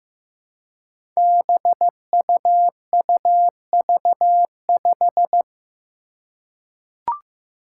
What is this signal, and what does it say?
Morse code, heard as a keyed tone, about 15 wpm, 700 Hz. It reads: BUUV5
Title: Five Challenging Letter-Number Combinations 15wpm